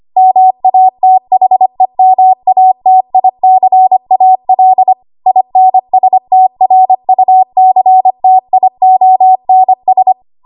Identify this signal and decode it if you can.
Morse code, heard as a keyed tone, about 25 wpm, 750 Hz. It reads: MATHEMATICAL INSTRUCTIONS
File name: Archive / 25 WPM / 25WPM